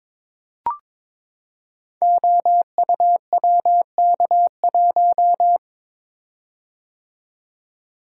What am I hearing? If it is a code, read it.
Morse code, heard as a keyed tone, about 23 wpm, 700 Hz. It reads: OUWK1
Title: Code Groups - 5 Letter:Numbers - 22wpm